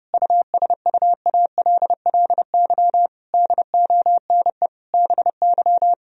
Morse code, heard as a keyed tone, about 30 wpm, 700 Hz. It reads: USUALLY DONE BY
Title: Sentences from Top 1000 Words - Review Part 3 - 30wpm